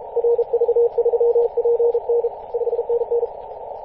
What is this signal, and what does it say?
Morse code, heard as a keyed tone, about 35 wpm, 470 Hz, with radio noise. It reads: RV3PN 5NN
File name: posts / 2016-may-01-0643z-a91hi-21-040mhz-cw